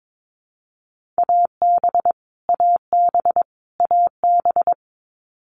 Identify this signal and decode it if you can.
Morse code, heard as a keyed tone, about 22 wpm, 700 Hz. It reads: AB AB AB
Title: Canadian Provinces and Territories - Repeated 3x - 22wpm